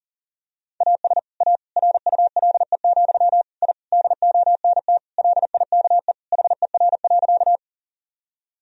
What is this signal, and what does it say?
Morse code, heard as a keyed tone, about 40 wpm, 700 Hz. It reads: AS A RULE, I DONT LIKE HER.
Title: Sentences from Top 1000 Words - Review Part 2 - 40wpm